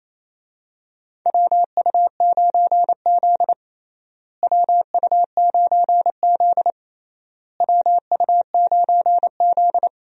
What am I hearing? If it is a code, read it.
Morse code, heard as a keyed tone, about 28 wpm, 700 Hz. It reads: WU9Z WU9Z WU9Z
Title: USA Call Signs - 4 Characters - Repeated 3x - 28wpm